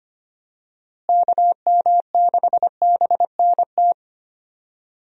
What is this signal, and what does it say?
Morse code, heard as a keyed tone, about 25 wpm, 700 Hz. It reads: KM6BNT
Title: USA Call Signs - 6 Characters - Repeated 2x - 25wpm